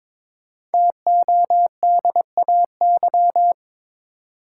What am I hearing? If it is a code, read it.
Morse code, heard as a keyed tone, about 22 wpm, 700 Hz. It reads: TODAY
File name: 5 Letter Words - First 3 Letters Given - 22wpm